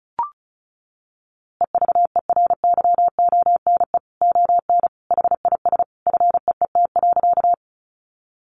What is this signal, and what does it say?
Morse code, heard as a keyed tone, about 34 wpm, 700 Hz. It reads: EVERYONE ON HIS FEET.
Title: A Sound Of Thunder 35wpm